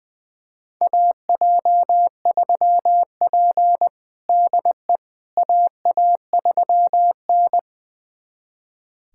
Morse code, heard as a keyed tone, about 20 wpm, 700 Hz. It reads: AJ3P DE AA3N
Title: USA Callsign DE Callsign - 4 Characters - 20wpm